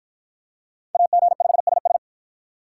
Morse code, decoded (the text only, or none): AG5SS